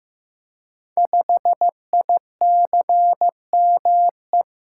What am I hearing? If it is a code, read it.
Morse code, heard as a keyed tone, about 15 wpm, 700 Hz. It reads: 5ICME
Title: Code Groups - 5 Letter:Numbers - 15wpm